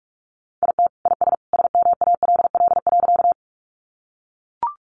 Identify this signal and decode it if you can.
Morse code, heard as a keyed tone, about 44 wpm, 700 Hz. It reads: IT IS SMALL.
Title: Easy Phrases 45wpm